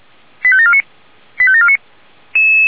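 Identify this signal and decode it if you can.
Five-tone selective call (ZVEI), heard as quick runs of five tones, tones of about 70 ms. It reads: 85749 85749
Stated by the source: ZVEI five-tone call received by SDR.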